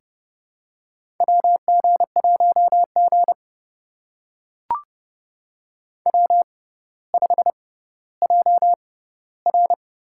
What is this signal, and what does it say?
Morse code, heard as a keyed tone, about 30 wpm, spaced as for 15 wpm, 700 Hz. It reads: WG1G W5JR
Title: USA Call Signs - 4 Characters - 6x Character Spacing - 30wpm